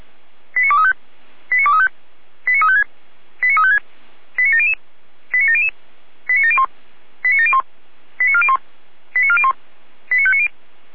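Five-tone selective call (ZVEI), heard as quick runs of five tones, tones of about 70 ms. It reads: 89235 89235 89356 89356 89700 89700 89701 89701 89401 89401 89590
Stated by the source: ZVEI five-tone call received on an SDR, from the feuerwehr-orlamuende.org archive.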